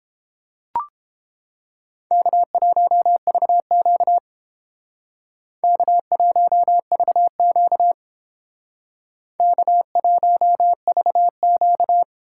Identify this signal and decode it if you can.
Morse code, heard as a keyed tone, about 29 wpm, 700 Hz. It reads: K1VQ K1VQ K1VQ